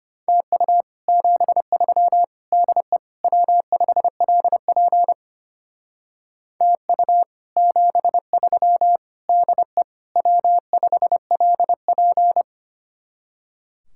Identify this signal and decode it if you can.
Morse code, heard as a keyed tone, about 31 wpm, 700 Hz. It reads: TU 73 DE W5LP TU 73 DE W5LP
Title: Speed-Racing Common QSO Phrases - 25wpm